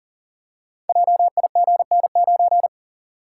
Morse code, heard as a keyed tone, about 39 wpm, 700 Hz. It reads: JIGN9
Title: Code Groups - 5 Letter:Numbers - 40wpm